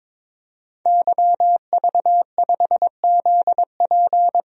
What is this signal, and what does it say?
Morse code, heard as a keyed tone, about 22 wpm, 700 Hz. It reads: YV5ZP
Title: DXCC-100 Call Signs - 5 Characters - 22wpm